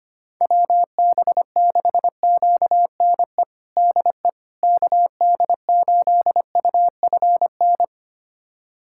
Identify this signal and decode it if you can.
Morse code, heard as a keyed tone, about 25 wpm, 700 Hz. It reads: WB6QNE DE KD8UFN
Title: USA Callsign DE Callsign - 6 Characters - 25wpm